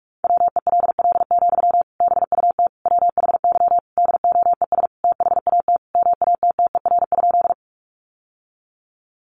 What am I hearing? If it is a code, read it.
Morse code, heard as a keyed tone, about 44 wpm, 700 Hz. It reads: WELL, BUT WHY DOES THAT MATTER?
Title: Sentences from Top 700 Words 45wpm